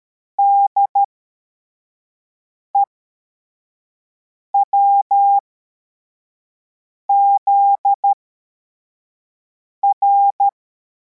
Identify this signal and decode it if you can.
Morse code, heard as a keyed tone, about 13 wpm, spaced as for 4 wpm, 800 Hz. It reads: DEWZR
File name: formation / cw / Text64